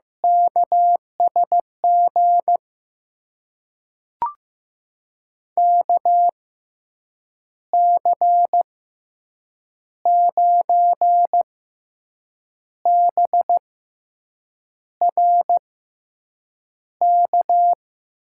Morse code, heard as a keyed tone, about 15 wpm, spaced as for 6 wpm, 700 Hz. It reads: KSG KC9BRK
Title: USA Call Signs - 6 Characters - 6x Character Spacing - 15wpm